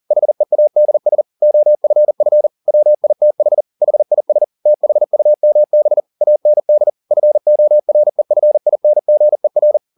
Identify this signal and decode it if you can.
Morse code, heard as a keyed tone, about 40 wpm, 600 Hz. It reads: HEADS OFF WITH HIS THUMB AND FOREFINGER
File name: the old man and the sea 2 40wpm